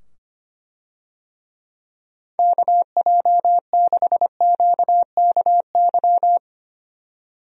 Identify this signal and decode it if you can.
Morse code, heard as a keyed tone, about 25 wpm, 700 Hz. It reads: KJ6QKY
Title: USA Call Signs - 6 Characters - 25wpm